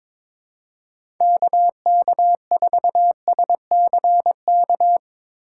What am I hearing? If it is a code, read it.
Morse code, heard as a keyed tone, about 21 wpm, 700 Hz. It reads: KK4SCK
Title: USA Call Signs - 6 Characters - 4x Character Spacing - 22wpm